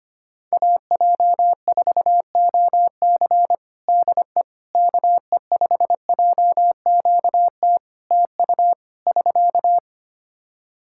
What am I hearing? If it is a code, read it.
Morse code, heard as a keyed tone, about 25 wpm, 700 Hz. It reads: AJ4OC DE KE5JQT TU <SK>
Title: K1USN SST QSO Practice 25wpm